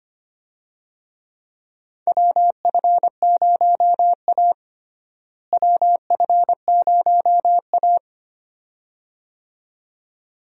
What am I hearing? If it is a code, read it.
Morse code, heard as a keyed tone, about 25 wpm, 700 Hz. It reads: WF0A WF0A
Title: USA Call Signs - 4 Characters - Repeated 2x - 25wpm